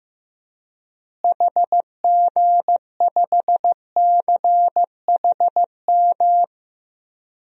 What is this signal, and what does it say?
Morse code, heard as a keyed tone, about 15 wpm, 700 Hz. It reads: HG5CHM
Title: DXCC-100 Call Signs - 6 Characters - Repeated 3x - 15wpm